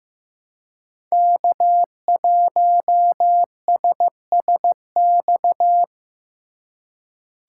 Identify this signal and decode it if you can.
Morse code, heard as a keyed tone, about 15 wpm, 700 Hz. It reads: K1SSX
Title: Code Groups - 5 Letter:Numbers - 15wpm